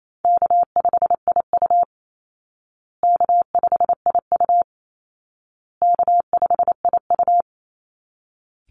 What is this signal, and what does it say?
Morse code, heard as a keyed tone, about 28 wpm, 700 Hz. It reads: K5IU K5IU K5IU
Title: Call Signs Repeated 2x - 28wpm